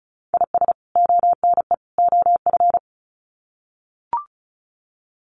Morse code, heard as a keyed tone, about 34 wpm, 700 Hz. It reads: IS ONE OF
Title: Sets of 3 Words 35wpm